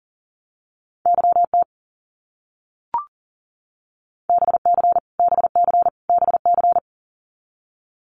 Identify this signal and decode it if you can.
Morse code, heard as a keyed tone, about 40 wpm, 700 Hz. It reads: YT BC BC BC
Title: Canadian Provinces and Territories - Repeated 3x - 40wpm